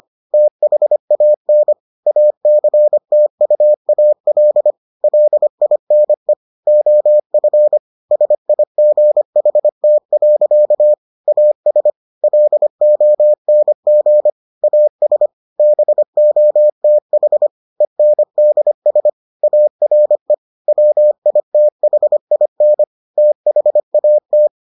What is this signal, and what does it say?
Morse code, heard as a keyed tone, about 25 wpm, 600 Hz. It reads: THAN ACTUAL LINE OF SIGHT. AS LONG AS BOTH ENDS ARE WITHIN THAT